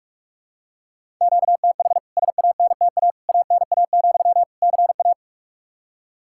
Morse code, heard as a keyed tone, about 41 wpm, 700 Hz. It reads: QTH SANTA ANA, CA
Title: K1USN SST QSO Practice 45wpm